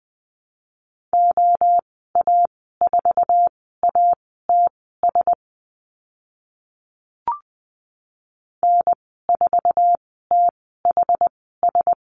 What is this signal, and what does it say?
Morse code, heard as a keyed tone, about 20 wpm, spaced as for 14 wpm, 700 Hz. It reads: OA4ATS N4THS
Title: 082 Lesson 14g - Number 2 - Call Signs 20wpm